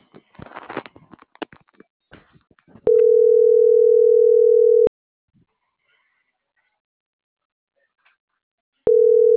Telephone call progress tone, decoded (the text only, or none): ringback tone